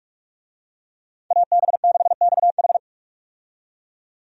Morse code, heard as a keyed tone, about 41 wpm, 700 Hz. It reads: AB6XH
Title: USA Call Signs - 5 Characters - 45wpm